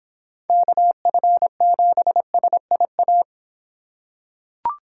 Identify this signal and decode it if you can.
Morse code, heard as a keyed tone, about 26 wpm, 700 Hz. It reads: KF7SIA